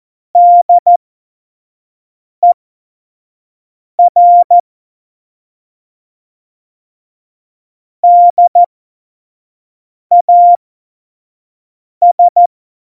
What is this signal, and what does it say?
Morse code, heard as a keyed tone, about 14 wpm, spaced as for 5 wpm, 700 Hz. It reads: DER DAS
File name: Morsekurs / Witze / Witz 13. Oktober 2020, 25 F70